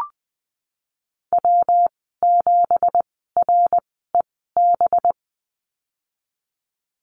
Morse code, heard as a keyed tone, about 20 wpm, spaced as for 14 wpm, 700 Hz. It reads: W7REB
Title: 193 Lesson 31g - Letter v - Call Signs 20wpm